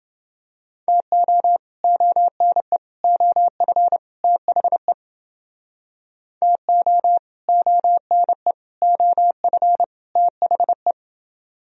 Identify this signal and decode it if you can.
Morse code, heard as a keyed tone, about 31 wpm, 700 Hz. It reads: TO ONE OF THE TO ONE OF THE